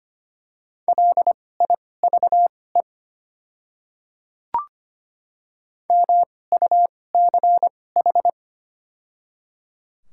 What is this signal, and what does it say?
Morse code, heard as a keyed tone, about 25 wpm, spaced as for 18 wpm, 700 Hz. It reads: LIVE MUCH
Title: 2x Character Spacing - Top 200 Words - 25wpm